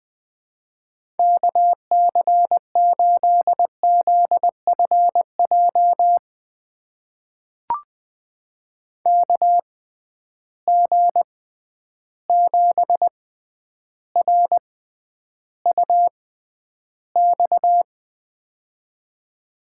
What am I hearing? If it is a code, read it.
Morse code, heard as a keyed tone, about 19 wpm, spaced as for 10 wpm, 700 Hz. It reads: KC8ZFJ KG7RUX
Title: USA Call Signs - 6 Characters - 6x Character Spacing - 20wpm